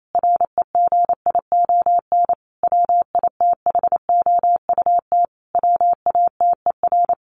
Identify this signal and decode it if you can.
Morse code, heard as a keyed tone, about 28 wpm, 700 Hz. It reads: REGION WITHOUT WATER
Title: Sentences from Top 1500 Words 28wpm